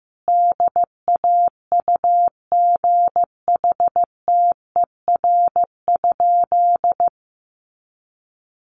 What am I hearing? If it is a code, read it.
Morse code, heard as a keyed tone, about 15 wpm, 700 Hz. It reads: DAUGHTER?
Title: Sentences from Top 1500 Words 15wpm